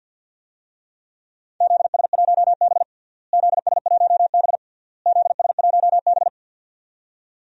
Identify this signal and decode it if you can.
Morse code, heard as a keyed tone, about 48 wpm, 700 Hz. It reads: ZS1B ZS1B ZS1B
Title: DXCC-100 Call Signs - 4 Characters - Repeated 3x - 50wpm